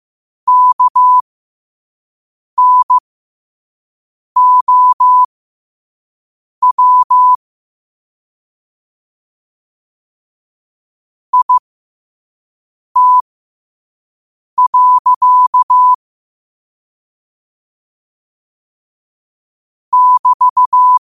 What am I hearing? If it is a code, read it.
Morse code, heard as a keyed tone, about 15 wpm, spaced as for 5 wpm, 1000 Hz. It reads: KNOW IT. <BT>
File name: qotdfiles / 5wpm